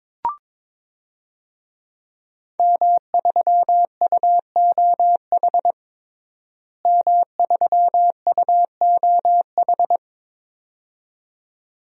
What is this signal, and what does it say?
Morse code, heard as a keyed tone, about 22 wpm, 700 Hz. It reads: M3UOH M3UOH
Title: DXCC-100 Call Signs - 5 Characters - Repeated 2x - 22wpm